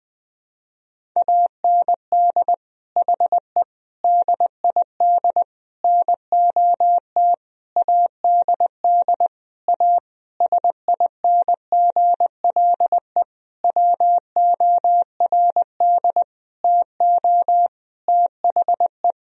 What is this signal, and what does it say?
Morse code, heard as a keyed tone, about 20 wpm, 700 Hz. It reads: AND HE DID NOT ADD A SINGLE WORD TO THE
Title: Sentences from Top 1000 Words - Review Part 1 - 20wpm